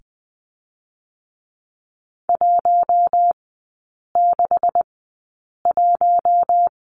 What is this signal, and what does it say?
Morse code, heard as a keyed tone, about 20 wpm, spaced as for 8 wpm, 700 Hz. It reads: 161
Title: 198 Lesson 32e - Letter k - Three Characters 20wpm